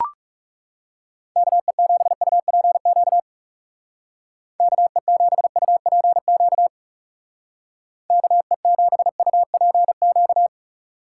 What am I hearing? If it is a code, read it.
Morse code, heard as a keyed tone, about 36 wpm, 700 Hz. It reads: KE7UPQ KE7UPQ KE7UPQ
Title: Speed Racing - USA 2x3 Call Signs 30wpm